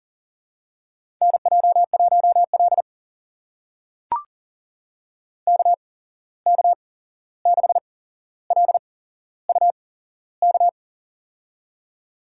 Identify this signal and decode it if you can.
Morse code, heard as a keyed tone, about 40 wpm, spaced as for 15 wpm, 700 Hz. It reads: NJ1L KK6LUK
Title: USA Call Signs - 8x Character Spacing - 40wpm